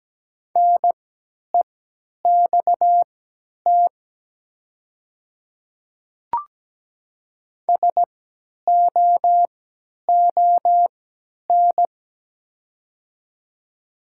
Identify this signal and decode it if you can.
Morse code, heard as a keyed tone, about 17 wpm, spaced as for 10 wpm, 700 Hz. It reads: NEXT SOON
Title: 3x Character Spacing - Top 500 Words - 17wpm